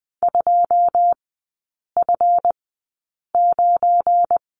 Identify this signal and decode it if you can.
Morse code, heard as a keyed tone, about 20 wpm, spaced as for 8 wpm, 700 Hz. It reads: 2F9
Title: 198 Lesson 32e - Letter k - Three Characters 20wpm